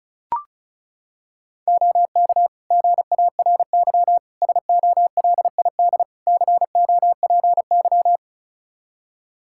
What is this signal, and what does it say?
Morse code, heard as a keyed tone, about 34 wpm, 700 Hz. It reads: OK GARY SOLID COPY
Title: Practice QSO - Exchange 3 - More Info - 35wpm